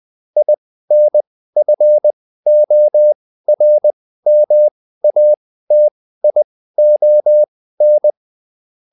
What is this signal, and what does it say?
Morse code, heard as a keyed tone, about 20 wpm, spaced as for 14 wpm, 600 Hz. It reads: INFORMATION